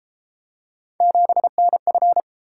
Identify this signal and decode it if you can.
Morse code, heard as a keyed tone, about 33 wpm, 700 Hz. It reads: 7NF